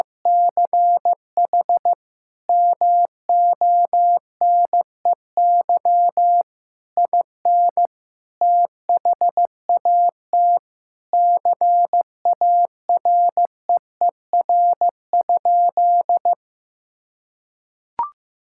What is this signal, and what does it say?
Morse code, heard as a keyed tone, about 15 wpm, 700 Hz. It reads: CH MONEY IN THAT CAREER?
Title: Sentences from Top 1000 Words - Review Part 6 - 15wpm